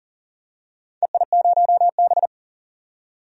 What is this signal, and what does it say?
Morse code, heard as a keyed tone, about 40 wpm, 700 Hz. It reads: EI0B
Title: DXCC-100 Call Signs - 4 Characters - 40wpm